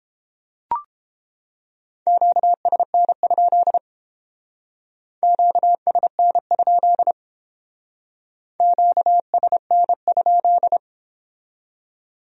Morse code, heard as a keyed tone, about 29 wpm, 700 Hz. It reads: QSN? QSN? QSN?